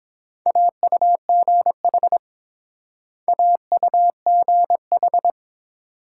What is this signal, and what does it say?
Morse code, heard as a keyed tone, about 30 wpm, 700 Hz. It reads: AUGH AUGH